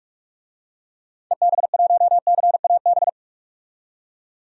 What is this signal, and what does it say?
Morse code, heard as a keyed tone, about 46 wpm, 700 Hz. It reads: EB1CAB
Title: DXCC-100 Call Signs - 6 Characters - 45wpm